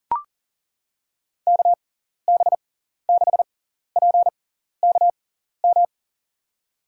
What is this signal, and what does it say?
Morse code, heard as a keyed tone, about 40 wpm, spaced as for 14 wpm, 700 Hz. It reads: KB6PKM
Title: USA Call Signs - 6 Characters - 6x Character Spacing - 40wpm